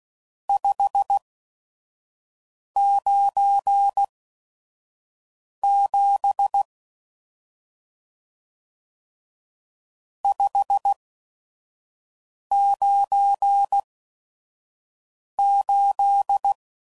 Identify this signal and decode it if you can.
Morse code, heard as a keyed tone, about 16 wpm, spaced as for 5 wpm, 800 Hz. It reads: 597 598
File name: CW Code Groups @5WPM / RST's / Various RST No's